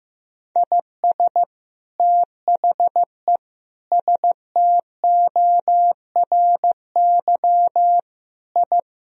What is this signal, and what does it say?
Morse code, heard as a keyed tone, about 15 wpm, 700 Hz. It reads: IS THE STORY I
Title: Sentences from Top 1000 Words - Review Part 2 - 15wpm